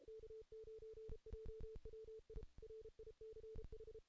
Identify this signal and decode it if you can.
Morse code, heard as a keyed tone, about 30 wpm, 440 Hz. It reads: K9JWI RICH
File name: 2016 JAN CW NAQP / 2016 Jan naqp k5ot - 0-8